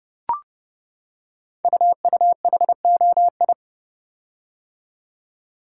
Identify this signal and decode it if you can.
Morse code, heard as a keyed tone, about 30 wpm, 700 Hz. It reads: UUHOI
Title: Code Groups - 5 Letters - 30wpm